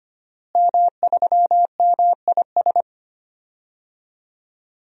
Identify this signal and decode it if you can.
Morse code, heard as a keyed tone, about 25 wpm, 700 Hz. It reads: M3MIS
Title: DXCC-100 Call Signs - 5 Characters - Repeated 2x - 25wpm